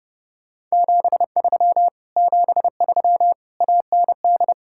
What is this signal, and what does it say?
Morse code, heard as a keyed tone, about 40 wpm, spaced as for 35 wpm, 700 Hz. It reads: 73 73 AND
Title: Speed-Racing Common QSO Phrases - 30wpm